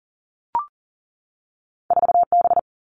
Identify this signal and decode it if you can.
Morse code, heard as a keyed tone, about 38 wpm, 700 Hz. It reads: 4 B
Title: Two Challenging Letter-Number Combinations 40wpm